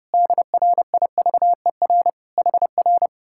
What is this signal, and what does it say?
Morse code, heard as a keyed tone, about 28 wpm, 700 Hz. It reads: DRIVER HR
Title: Speed-Racing Common QSO Phrases Encore - 30wpm